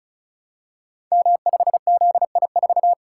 Morse code, heard as a keyed tone, about 36 wpm, 700 Hz. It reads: M5ZI4
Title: Code Groups - 5 Letter:Numbers - 35wpm